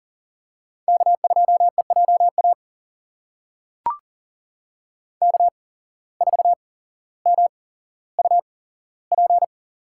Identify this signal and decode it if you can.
Morse code, heard as a keyed tone, about 40 wpm, spaced as for 17 wpm, 700 Hz. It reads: K2EJA K4MUP
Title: USA Call Signs - 5 Characters - 8x Character Spacing - 40wpm